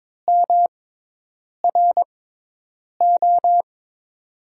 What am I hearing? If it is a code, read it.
Morse code, heard as a keyed tone, about 22 wpm, spaced as for 8 wpm, 700 Hz. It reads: MRO
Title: USA Call Signs - 6x Character Spacing - 22wpm